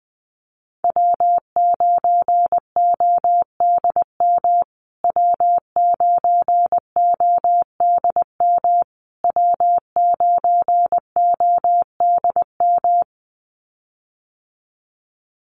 Morse code, heard as a keyed tone, about 20 wpm, 700 Hz. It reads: W9ODM W9ODM W9ODM
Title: USA Call Signs Repeated 3x 20wpm